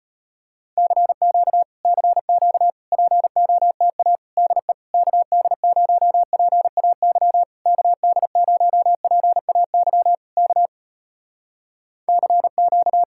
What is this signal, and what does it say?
Morse code, heard as a keyed tone, about 34 wpm, 700 Hz. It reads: CQ CQ POTA DE KD0PAY KD0PAY K CQ